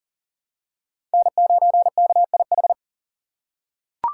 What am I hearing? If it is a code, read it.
Morse code, heard as a keyed tone, about 40 wpm, 700 Hz. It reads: N9KIH
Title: USA Call Signs - 5 Characters - 6x Character Spacing - 40wpm